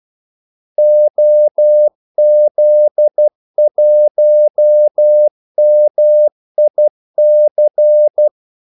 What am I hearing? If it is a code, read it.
Morse code, heard as a keyed tone, about 12 wpm, 600 Hz. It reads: OZ1MIC
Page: 2 ekstra ordmellemrum mellem kaldesignalerne